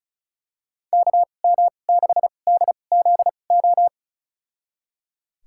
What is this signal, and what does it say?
Morse code, heard as a keyed tone, about 34 wpm, spaced as for 25 wpm, 700 Hz. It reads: KM6DZO
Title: USA Call Signs - 2x Character Spacing - 35wpm